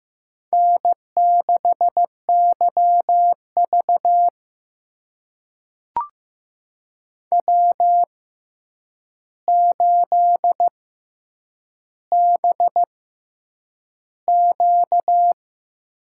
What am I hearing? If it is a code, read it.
Morse code, heard as a keyed tone, about 15 wpm, spaced as for 8 wpm, 700 Hz. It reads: N6YV W8BQ
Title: USA Call Signs - 4 Characters - 6x Character Spacing - 15wpm